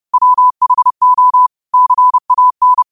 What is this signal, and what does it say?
Morse code, heard as a keyed tone, about 30 wpm, 1000 Hz. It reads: WHO CAN
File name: qotdfiles / 30wpm